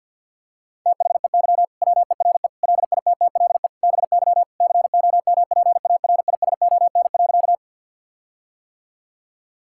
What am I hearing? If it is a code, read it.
Morse code, heard as a keyed tone, about 50 wpm, 700 Hz. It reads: THEY WERE LITTLE BY COMPARISON.
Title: Sentences from Top 1600 Words 50wpm